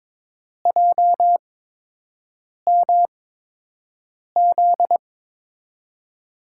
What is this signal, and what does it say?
Morse code, heard as a keyed tone, about 23 wpm, spaced as for 6 wpm, 700 Hz. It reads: JMZ
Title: USA Call Signs - 6 Characters - 8x Character Spacing - 22wpm